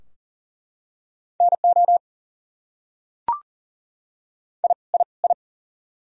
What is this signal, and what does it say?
Morse code, heard as a keyed tone, about 38 wpm, 700 Hz. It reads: NO I I I
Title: Top 100 Words Repeated 40wpm